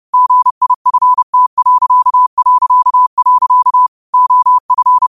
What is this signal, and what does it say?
Morse code, heard as a keyed tone, about 30 wpm, 1000 Hz. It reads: GIFT... OF
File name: qotdfiles / 30wpm